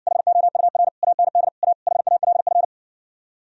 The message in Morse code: HOUR AND A HALF

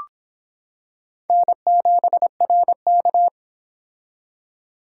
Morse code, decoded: N7RK

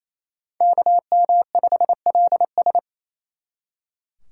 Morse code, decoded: KM5LS

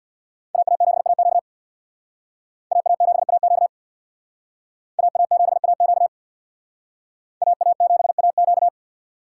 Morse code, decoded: AA7AQ AA7AQ AA7AQ AA7AQ

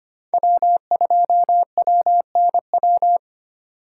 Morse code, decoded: W2WNW

